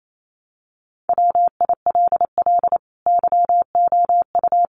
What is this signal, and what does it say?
Morse code, heard as a keyed tone, about 28 wpm, 700 Hz. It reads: WILL YOU
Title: Sentences from Top 400 Words 28wpm